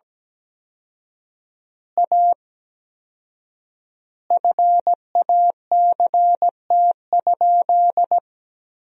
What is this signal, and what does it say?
Morse code, heard as a keyed tone, about 17 wpm, 700 Hz. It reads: A FACT?
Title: 4x Word-Spacing - Sentences from Top 500 Words - 17wpm